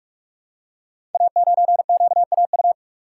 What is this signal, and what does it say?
Morse code, heard as a keyed tone, about 44 wpm, 700 Hz. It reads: A9QAU
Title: Code Groups - 5 Letter:Numbers - 45wpm